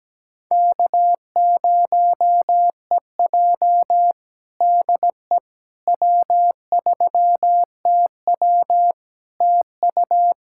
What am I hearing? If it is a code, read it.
Morse code, heard as a keyed tone, about 17 wpm, 700 Hz. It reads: K0EJ DE W3TW TU
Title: Practice QSO - Exchange 4 - Ending - 17wpm